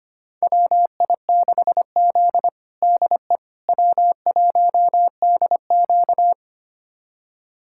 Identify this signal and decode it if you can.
Morse code, heard as a keyed tone, about 25 wpm, 700 Hz. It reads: WI6Z DE W1DQ
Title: USA Callsign DE Callsign - 4 Characters - 25wpm